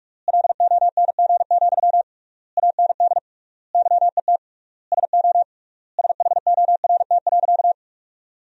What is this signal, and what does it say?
Morse code, heard as a keyed tone, about 47 wpm, 700 Hz. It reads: LONG, AND YET SO SHORT.
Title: 3x Word-Spacing - Sentences from Top 500 Words - 45wpm